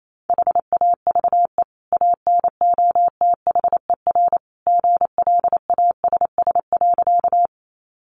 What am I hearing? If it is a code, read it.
Morse code, heard as a keyed tone, about 28 wpm, 700 Hz. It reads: HAVE ANOTHER GLASS.
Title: Sentences from Top 1300 Words 28wpm